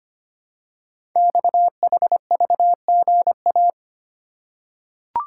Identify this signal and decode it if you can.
Morse code, heard as a keyed tone, about 25 wpm, 700 Hz. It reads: XHVGA
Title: Code Groups - 5 Letters - 25wpm